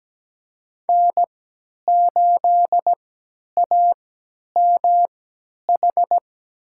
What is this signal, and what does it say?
Morse code, heard as a keyed tone, about 17 wpm, spaced as for 10 wpm, 700 Hz. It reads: N8AMH
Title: USA Call Signs - 3x Character Spacing - 17wpm